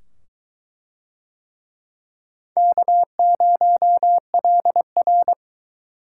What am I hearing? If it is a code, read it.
Morse code, heard as a keyed tone, about 20 wpm, 700 Hz. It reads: K0LR